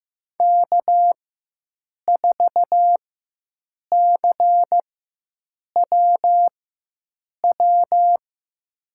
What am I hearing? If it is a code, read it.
Morse code, heard as a keyed tone, about 15 wpm, spaced as for 7 wpm, 700 Hz. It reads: K4CWW
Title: USA Call Signs - 5 Characters - 4x Character Spacing - 15wpm